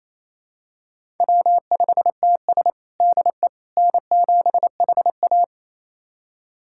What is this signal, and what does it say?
Morse code, heard as a keyed tone, about 27 wpm, 700 Hz. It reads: W5TS DE N7HA
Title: USA Callsign DE Callsign - 4 Characters - 28wpm